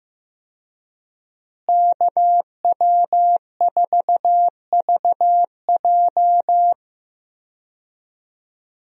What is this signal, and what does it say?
Morse code, heard as a keyed tone, about 15 wpm, 700 Hz. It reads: KW4VJ
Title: USA Call Signs - 5 Characters - Repeated 3x - 15wpm